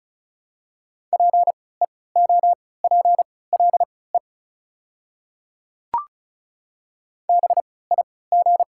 Morse code, heard as a keyed tone, about 35 wpm, spaced as for 20 wpm, 700 Hz. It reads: PEOPLE BIG
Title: 3x Character Spacing - Top 200 Words - 35wpm